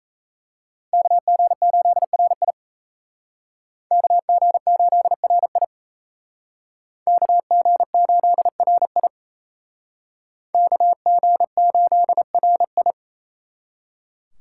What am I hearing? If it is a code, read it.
Morse code, heard as a keyed tone, about 36 wpm, 700 Hz. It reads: KG8RI KG8RI KG8RI KG8RI